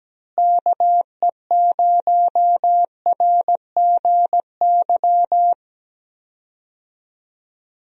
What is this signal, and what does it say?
Morse code, heard as a keyed tone, about 17 wpm, 700 Hz. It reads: KE0RGY
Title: USA Call Signs - 6 Characters - Repeated 2x - 17wpm